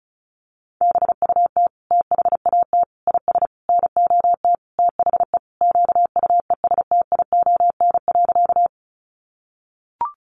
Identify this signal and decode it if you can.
Morse code, heard as a keyed tone, about 34 wpm, 700 Hz. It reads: BUT THAT IS NOT THE QUESTION.
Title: Sentences from Top 600 Words 35wpm